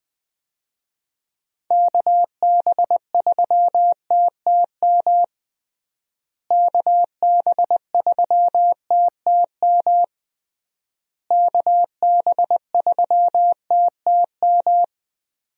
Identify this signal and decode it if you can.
Morse code, heard as a keyed tone, about 20 wpm, 700 Hz. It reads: KB3TTM KB3TTM KB3TTM
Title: USA Call Signs - 6 Characters - Repeated 3x - 20wpm